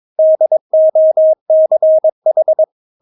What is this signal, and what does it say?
Morse code, heard as a keyed tone, about 22 wpm, 630 Hz. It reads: DOCH
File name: Witz3 22wpm